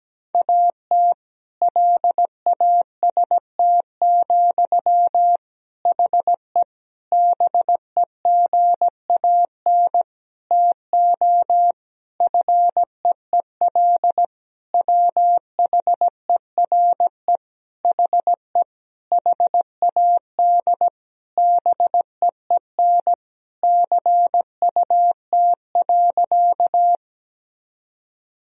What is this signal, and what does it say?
Morse code, heard as a keyed tone, about 17 wpm, 700 Hz. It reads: AT LAST, HE BEGAN TO FEEL WHERE HE HAD BEEN CUT.
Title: Sentences from Top 1000 Words - Review Part 1 - 17wpm